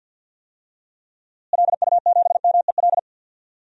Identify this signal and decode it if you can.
Morse code, heard as a keyed tone, about 51 wpm, 700 Hz. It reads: LU7MEL
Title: DXCC-100 Call Signs - 6 Characters - Repeated 3x - 50wpm